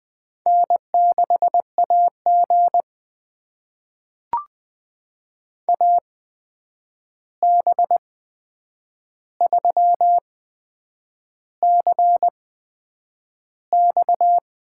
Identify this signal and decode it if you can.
Morse code, heard as a keyed tone, about 20 wpm, spaced as for 8 wpm, 700 Hz. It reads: N6AG AB3CX
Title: USA Call Signs - 8x Character Spacing - 20wpm